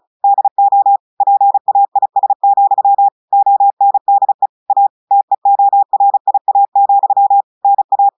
Morse code, heard as a keyed tone, about 36 wpm, 800 Hz. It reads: DO PAIS, ONDE A TEORIA, NA
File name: audios / portugais / portugais texte en clair 1 vitesse 35 mots minutes